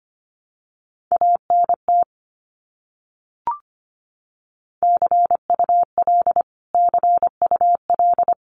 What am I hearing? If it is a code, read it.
Morse code, heard as a keyed tone, about 25 wpm, 700 Hz. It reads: ANT CUL CUL
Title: Abbreviations Repeated 3x - 25wpm